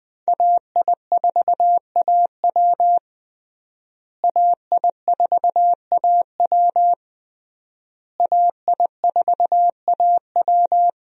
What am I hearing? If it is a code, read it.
Morse code, heard as a keyed tone, about 21 wpm, 700 Hz. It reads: AI4AW AI4AW AI4AW
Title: USA Call Signs - 5 Characters - Repeated 3x - 20wpm